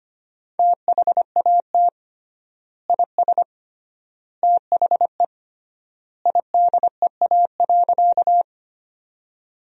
Morse code, heard as a keyed tone, about 25 wpm, 700 Hz. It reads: THAT IS THE IDEA.
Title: 3x Word-Spacing - Sentences from Top 300 Words - 25wpm